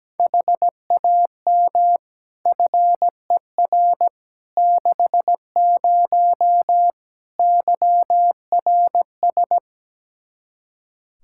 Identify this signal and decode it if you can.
Morse code, heard as a keyed tone, about 17 wpm, 700 Hz. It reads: HAM FER 60 YRS
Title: Practice QSO - Exchange 3 - More Info - 17wpm